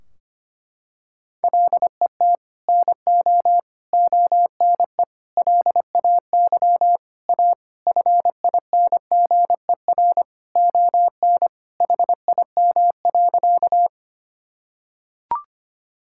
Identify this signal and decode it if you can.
Morse code, heard as a keyed tone, about 25 wpm, 700 Hz. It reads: LET NO ONE LAY A FINGER ON HIM.
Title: Sentences from Top 1600 Words 25wpm